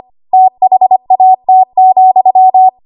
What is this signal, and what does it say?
Morse code, heard as a keyed tone, about 25 wpm, 750 Hz. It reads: THAT,
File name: Archive / 25 WPM / 25WPM